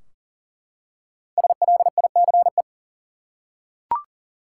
Morse code, heard as a keyed tone, about 40 wpm, 700 Hz. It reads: SLICE
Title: Five Letter Words - 40wpm